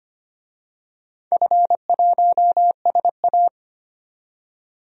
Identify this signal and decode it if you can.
Morse code, heard as a keyed tone, about 25 wpm, 700 Hz. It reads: F1SA
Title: DXCC-100 Call Signs - 4 Characters - 25wpm